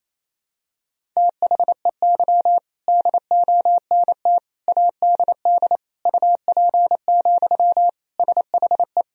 Morse code, heard as a keyed tone, about 28 wpm, 700 Hz. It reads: THEY DONT ADD UP, SHE
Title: Sentences from Top 1000 Words - Review Part 1 - 28wpm